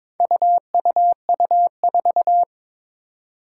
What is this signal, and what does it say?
Morse code, heard as a keyed tone, about 22 wpm, 700 Hz. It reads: UUU4
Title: Four Challenging Letter-Number Combinations 22wpm